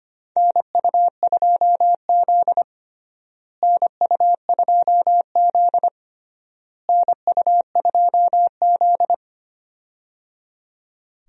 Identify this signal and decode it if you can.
Morse code, heard as a keyed tone, about 25 wpm, 700 Hz. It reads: NU2Z NU2Z NU2Z
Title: USA Call Signs - 4 Characters - Repeated 3x - 25wpm